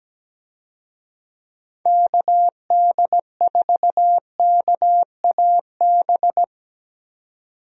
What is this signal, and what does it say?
Morse code, heard as a keyed tone, about 17 wpm, 700 Hz. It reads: KD4KAB
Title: USA Call Signs - 6 Characters - 17wpm